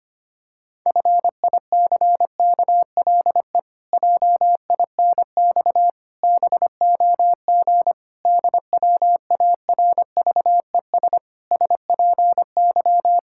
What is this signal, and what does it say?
Morse code, heard as a keyed tone, about 25 wpm, 700 Hz. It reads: FICKLE JINX BOG DWARVES SPY
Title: Pangram Challenge v2 - 25wpm